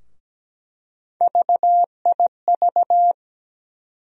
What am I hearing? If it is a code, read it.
Morse code, heard as a keyed tone, about 17 wpm, 700 Hz. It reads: VIV
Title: USA Call Sign Postfixes - 3 Letters 17wpm